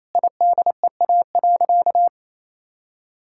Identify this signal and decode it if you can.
Morse code, heard as a keyed tone, about 28 wpm, 700 Hz. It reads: IDEA.
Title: Sentences from Top 1000 Words - Review Part 6 - 28wpm